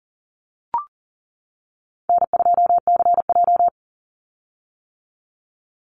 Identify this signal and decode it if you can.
Morse code, heard as a keyed tone, about 40 wpm, 700 Hz. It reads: N2CJ
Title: Call Signs - 40wpm